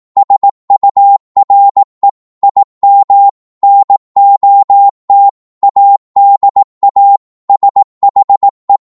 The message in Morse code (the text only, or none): SURE IM NOT ADA SHE